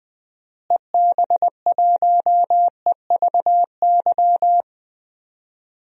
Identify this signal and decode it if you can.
Morse code, heard as a keyed tone, about 21 wpm, 700 Hz. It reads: EB1EVY
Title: DXCC-100 Call Signs - 6 Characters - Repeated 3x - 20wpm